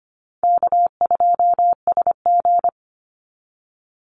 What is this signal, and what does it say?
Morse code, heard as a keyed tone, about 25 wpm, 700 Hz. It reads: K2SG
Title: Easy Call Signs Repeated 2x - 25wpm